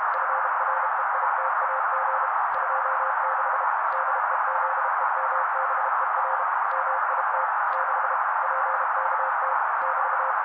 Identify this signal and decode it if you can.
Morse code, heard as a keyed tone, about 32 wpm, 530 Hz. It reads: FREITAG 26 DEZEMBER MIT 6 PKT <BT>
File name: daily / 31wpm noise